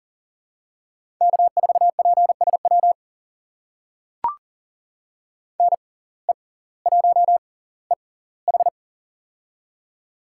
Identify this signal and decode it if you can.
Morse code, heard as a keyed tone, about 40 wpm, spaced as for 21 wpm, 700 Hz. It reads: K4PSW NE1EH
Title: USA Call Signs - 5 Characters - 6x Character Spacing - 40wpm